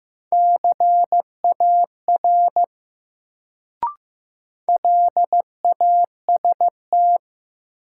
Morse code, heard as a keyed tone, about 15 wpm, 700 Hz. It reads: CAR LAST
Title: Sentences from Top 300 Words - New Word Drill - 15wpm